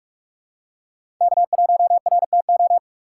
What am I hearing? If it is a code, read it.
Morse code, heard as a keyed tone, about 47 wpm, 700 Hz. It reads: K1RTO
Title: USA Call Signs - 6x Character Spacing - 45wpm